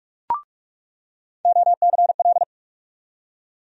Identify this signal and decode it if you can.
Morse code, heard as a keyed tone, about 47 wpm, 700 Hz. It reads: OCL
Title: USA Call Sign Postfixes - 3 Letters 45wpm